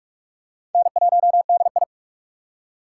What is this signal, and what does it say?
Morse code, heard as a keyed tone, about 46 wpm, 700 Hz. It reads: N1DI
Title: USA Call Signs - 4 Characters - 4x Character Spacing - 45wpm